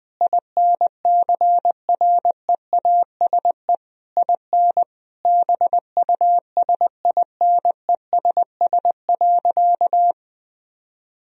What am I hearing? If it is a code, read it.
Morse code, heard as a keyed tone, about 21 wpm, 700 Hz. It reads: INCREASE IN BUSINESS.
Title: Sentences from Top 1000 Words - Review Part 4 - 20wpm